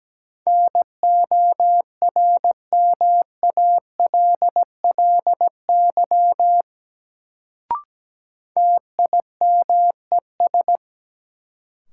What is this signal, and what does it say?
Morse code, heard as a keyed tone, about 17 wpm, 700 Hz. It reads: NORMALLY TIMES
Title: Sentences from Top 1000 Words - Review Part 6 - New Word Drill - 17wpm